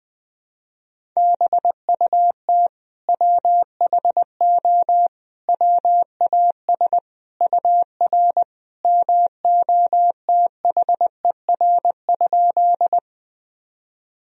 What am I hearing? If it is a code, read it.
Morse code, heard as a keyed tone, about 20 wpm, 700 Hz. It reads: BUT WHO WAS UR MOTHER?
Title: Sentences from Top 200 Words and Abbreviations 20wpm